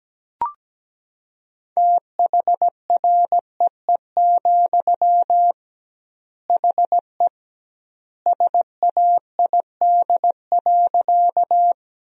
Morse code, heard as a keyed tone, about 17 wpm, 700 Hz. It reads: THREE, HE SAID.
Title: 2x Word-Spacing - Sentences from Top 200 Words - 17wpm